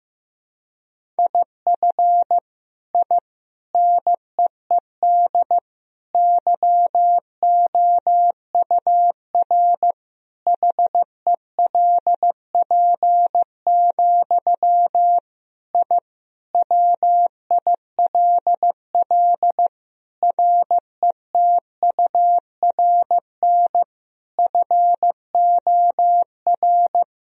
Morse code, heard as a keyed tone, about 15 wpm, 700 Hz. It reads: IF I NEED YOUR HELP, I WILL RETURN FOR
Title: Sentences from Top 1000 Words - Review Part 1 - 15wpm